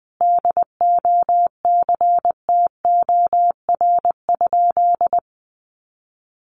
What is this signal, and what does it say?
Morse code, heard as a keyed tone, about 20 wpm, 700 Hz. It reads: DOCTOR?
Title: Sentences from Top 1500 Words 20wpm